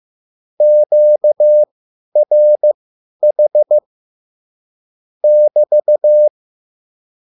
Morse code, heard as a keyed tone, about 15 wpm, spaced as for 10 wpm, 600 Hz. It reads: QRH <BT>
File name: QSL / CW lesson